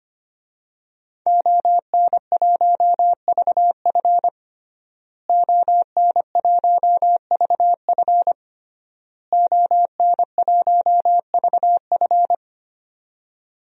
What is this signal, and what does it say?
Morse code, heard as a keyed tone, about 25 wpm, 700 Hz. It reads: ON1VF ON1VF ON1VF
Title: DXCC-100 Call Signs - 5 Characters - Repeated 3x - 25wpm